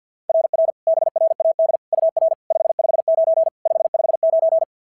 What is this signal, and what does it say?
Morse code, heard as a keyed tone, about 50 wpm, 690 Hz. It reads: RR BRAD UR 559 559
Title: SOTA QSO Practice 50wpm